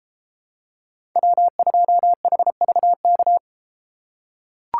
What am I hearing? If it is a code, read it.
Morse code, heard as a keyed tone, about 32 wpm, 700 Hz. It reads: W2HVK